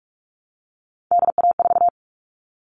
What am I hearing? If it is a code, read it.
Morse code, heard as a keyed tone, about 44 wpm, 700 Hz. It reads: DA4
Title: All Cut Numbers - 3-Digit 45wpm